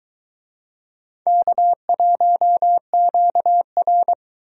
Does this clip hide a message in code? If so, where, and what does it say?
Yes: Morse code from 1.3 to 4.1 s, heard as a keyed tone, about 21 wpm, 700 Hz. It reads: K1QR